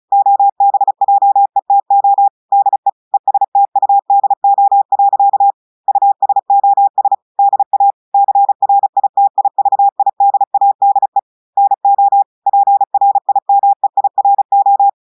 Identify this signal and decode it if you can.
Morse code, heard as a keyed tone, about 35 wpm, 800 Hz. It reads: OBJETO DE ESTUDO. USOS DA CRITIVIDADE NO PRIMEIRO
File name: audios / portugais / portugais texte en clair 1 vitesse 35 mots minutes